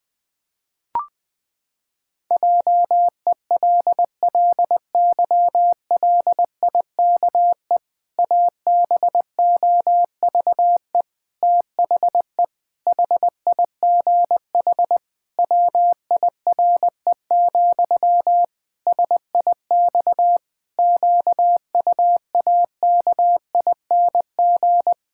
Morse code, heard as a keyed tone, about 20 wpm, 700 Hz. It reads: JELLYLIKE ABOVE THE HIGH WIRE, SIX QUAKING
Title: Pangram Challenge v2 - 20wpm